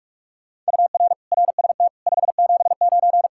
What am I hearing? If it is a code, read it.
Morse code, heard as a keyed tone, about 59 wpm, 700 Hz. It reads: UR RST 579